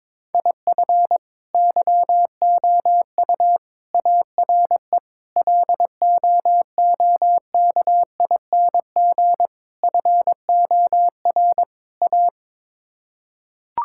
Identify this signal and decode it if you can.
Morse code, heard as a keyed tone, about 22 wpm, 700 Hz. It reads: IF YOU ARE LOOKING FOR A
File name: practice / 22wpm / Sets of 6 Words - 22wpm